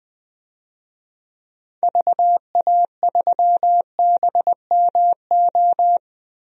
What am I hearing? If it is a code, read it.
Morse code, heard as a keyed tone, about 20 wpm, 700 Hz. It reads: VA3BMO
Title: DXCC-100 Call Signs - 6 Characters - 20wpm